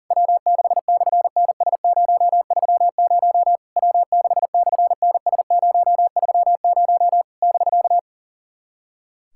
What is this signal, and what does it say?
Morse code, heard as a keyed tone, about 40 wpm, 700 Hz. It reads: W6/NS030 W6/NS030 <BK>
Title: SOTA QSO Practice 40wpm